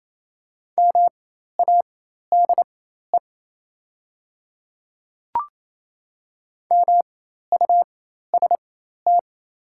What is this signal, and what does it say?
Morse code, heard as a keyed tone, about 28 wpm, spaced as for 13 wpm, 700 Hz. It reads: MADE MUST
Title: 4x Character Spacing - Top 300 Words - 28wpm